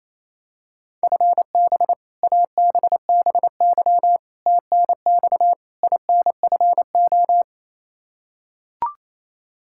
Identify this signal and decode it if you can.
Morse code, heard as a keyed tone, about 28 wpm, 700 Hz. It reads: FB ABBY TNX INFO
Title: Practice QSO - Exchange 3 - More Info - 28wpm